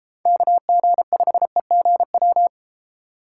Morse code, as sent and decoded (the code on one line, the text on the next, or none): -.- --. ..... --. .--
KG5EGW